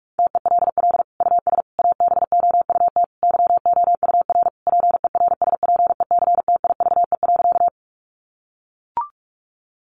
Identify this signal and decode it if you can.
Morse code, heard as a keyed tone, about 44 wpm, 700 Hz. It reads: TELL US ABOUT YOUR PERSPECTIVE.
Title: Sentences from Top 1200 Words 45wpm